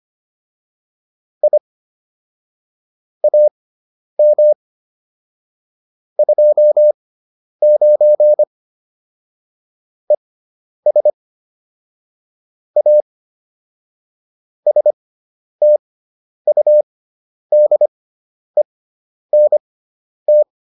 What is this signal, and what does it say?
Morse code, heard as a keyed tone, about 25 wpm, spaced as for 10 wpm, 600 Hz. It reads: I AM 29 ES A STUDENT
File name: QSO303 10